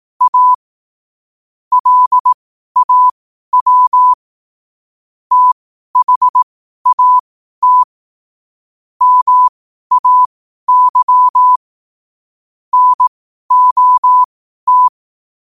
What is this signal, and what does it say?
Morse code, heard as a keyed tone, about 18 wpm, spaced as for 12 wpm, 1000 Hz. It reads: A LAW THAT MAY NOT
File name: qotdfiles / 12wpm